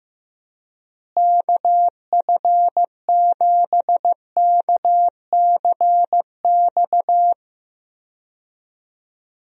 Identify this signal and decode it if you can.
Morse code, heard as a keyed tone, about 16 wpm, 700 Hz. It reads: KF7KCX
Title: USA Call Signs - 6 Characters - Repeated 3x - 15wpm